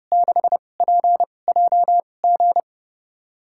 Morse code, heard as a keyed tone, about 30 wpm, spaced as for 22 wpm, 700 Hz. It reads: 6PJG
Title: USA Call Signs - 5 Characters - 2x Character Spacing - 30wpm